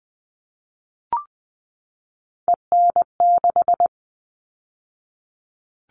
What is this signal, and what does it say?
Morse code, heard as a keyed tone, about 20 wpm, 700 Hz. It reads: EN6
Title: All Cut Numbers - 3-Digit 20wpm